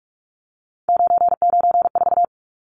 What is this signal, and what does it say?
Morse code, heard as a keyed tone, about 44 wpm, 700 Hz. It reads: 994
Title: Numbers - 3-Digit 45wpm